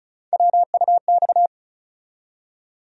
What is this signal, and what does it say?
Morse code, heard as a keyed tone, about 34 wpm, 700 Hz. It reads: WUX
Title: USA Call Sign Postfixes - 3 Letters 35wpm